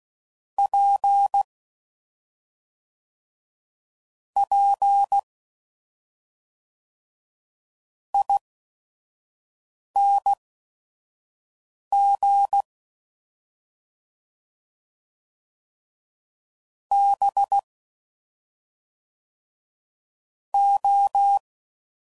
Morse code, heard as a keyed tone, about 16 wpm, spaced as for 5 wpm, 800 Hz. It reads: P P ING B O